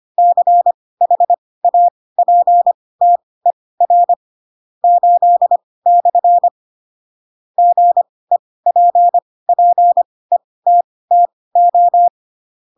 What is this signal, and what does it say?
Morse code, heard as a keyed tone, about 25 wpm, spaced as for 18 wpm, 700 Hz. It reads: CHAPTER 8/ GEPPETTO